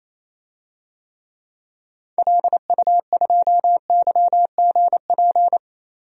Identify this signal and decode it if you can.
Morse code, heard as a keyed tone, about 28 wpm, 700 Hz. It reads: LU2YGP
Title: DXCC-100 Call Signs - 6 Characters - 28wpm